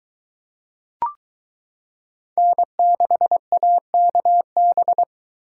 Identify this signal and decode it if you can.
Morse code, heard as a keyed tone, about 22 wpm, 700 Hz. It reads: N6AKB